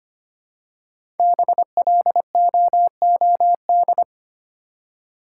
Morse code, heard as a keyed tone, about 25 wpm, 700 Hz. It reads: BLOOD
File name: 5 Letter Words - First 3 Letters Given - 25wpm